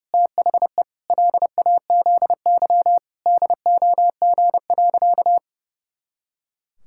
Morse code, heard as a keyed tone, about 30 wpm, 700 Hz. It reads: THE LAZY DOG.
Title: Pangram Challenge v2 - 30wpm